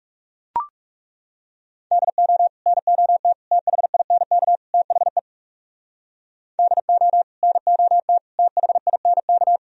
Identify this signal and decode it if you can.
Morse code, heard as a keyed tone, about 39 wpm, 700 Hz. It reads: DO NOT THINK THE DO NOT THINK